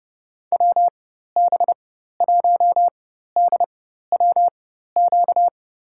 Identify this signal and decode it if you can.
Morse code, heard as a keyed tone, about 30 wpm, spaced as for 14 wpm, 700 Hz. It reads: WB1DWQ
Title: USA Call Signs - 6 Characters - 4x Character Spacing - 30wpm